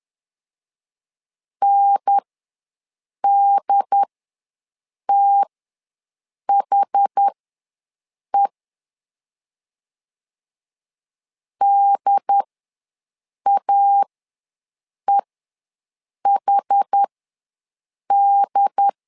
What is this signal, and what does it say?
Morse code, heard as a keyed tone, about 10 wpm, spaced as for 6 wpm, 800 Hz. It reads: NDTHE DAEHD
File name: LEZIONI / 2 UHDN / 2 UHDN ws 10wpm 2044